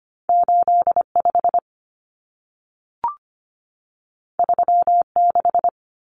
Morse code, heard as a keyed tone, about 25 wpm, 700 Hz. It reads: 85 36
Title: Common Cut Numbers - 2-Digit 25wpm